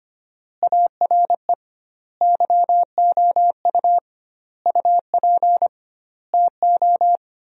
Morse code, heard as a keyed tone, about 25 wpm, 700 Hz. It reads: ARE YOU UP TO
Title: 2x Word-Spacing - Sentences from Top 100 Words - 25wpm